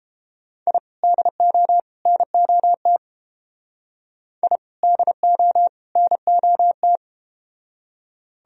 Morse code, heard as a keyed tone, about 31 wpm, 700 Hz. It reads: I DO NOT I DO NOT